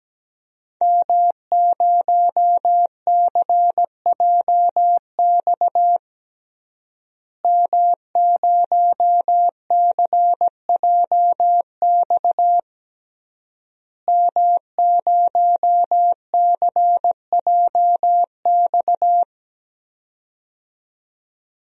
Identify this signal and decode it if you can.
Morse code, heard as a keyed tone, about 17 wpm, 700 Hz. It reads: M0CJX M0CJX M0CJX
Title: DXCC-100 Call Signs - 5 Characters - Repeated 3x - 17wpm